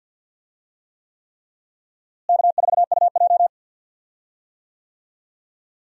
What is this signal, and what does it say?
Morse code, heard as a keyed tone, about 52 wpm, 700 Hz. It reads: K4UJ
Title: USA Call Signs - 4 Characters - Repeated 3x - 50wpm